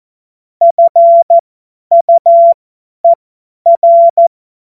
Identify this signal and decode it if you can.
Morse code, heard as a keyed tone, about 14 wpm, spaced as for 10 wpm, 680 Hz. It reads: FUER
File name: Morsekurs / Witze / Witz 16 Januar 2024 50 F70